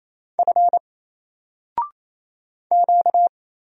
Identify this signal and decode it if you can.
Morse code, heard as a keyed tone, about 29 wpm, 700 Hz. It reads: F Q
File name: Single-Letter-Number-28wpm